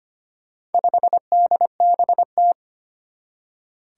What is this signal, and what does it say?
Morse code, heard as a keyed tone, about 25 wpm, 700 Hz. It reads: 5DBT
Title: Four Challenging Letter-Number Combinations 25wpm